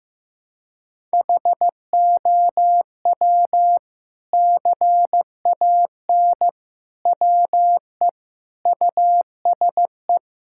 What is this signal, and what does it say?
Morse code, heard as a keyed tone, about 16 wpm, 700 Hz. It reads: HOW CAN WE USE
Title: Sentences from Top 1000 Words - Review Part 5 - 15wpm